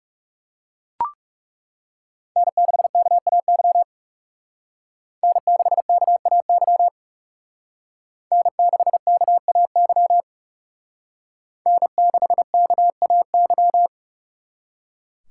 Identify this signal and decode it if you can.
Morse code, heard as a keyed tone, about 36 wpm, 700 Hz. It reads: N6KAY N6KAY N6KAY N6KAY
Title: Speed Racing - USA Call Signs 30wpm